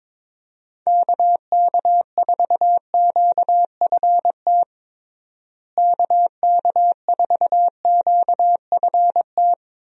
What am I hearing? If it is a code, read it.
Morse code, heard as a keyed tone, about 22 wpm, 700 Hz. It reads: KK4QFT KK4QFT
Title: USA Call Signs - 6 Characters - Repeated 3x - 22wpm